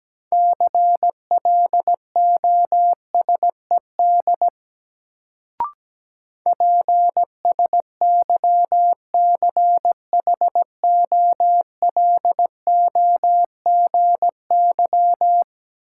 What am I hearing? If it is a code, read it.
Morse code, heard as a keyed tone, about 17 wpm, 700 Hz. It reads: CLOSED PSYCHOLOGY
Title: Sentences from Top 1400 Words - New Word Drill - 17wpm